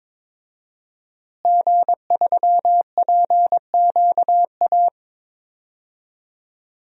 Morse code, heard as a keyed tone, about 22 wpm, 700 Hz. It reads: G3PQA
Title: DXCC-100 Call Signs - 5 Characters - Repeated 2x - 22wpm